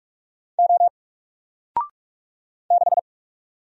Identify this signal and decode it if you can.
Morse code, heard as a keyed tone, about 44 wpm, 700 Hz. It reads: O 6
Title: Single Letter-Number 45wpm